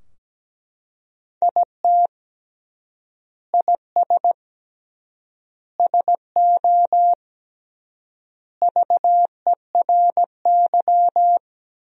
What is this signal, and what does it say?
Morse code, heard as a keyed tone, about 17 wpm, 700 Hz. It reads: IT IS SO VERY
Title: 3x Word-Spacing - Sentences from Top 300 Words - 17wpm